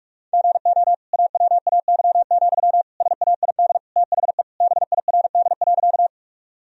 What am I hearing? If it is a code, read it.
Morse code, heard as a keyed tone, about 43 wpm, 700 Hz. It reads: GO AWAY, SAID THE BIRD.
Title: Sentences from Top 1000 Words - Review Part 2 - 45wpm